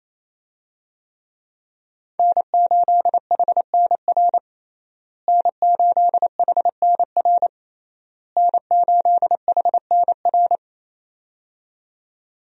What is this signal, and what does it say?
Morse code, heard as a keyed tone, about 27 wpm, 700 Hz. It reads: N8HNR N8HNR N8HNR
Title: USA Call Signs - 5 Characters - Repeated 3x - 28wpm